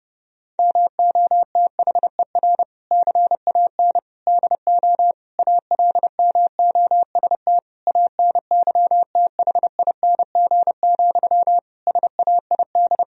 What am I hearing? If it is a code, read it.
Morse code, heard as a keyed tone, about 32 wpm, 700 Hz. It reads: MOTHER CAN DO ALMOST ANYTHING, SAID
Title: Sentences from Top 1000 Words - Review Part 4 - 30wpm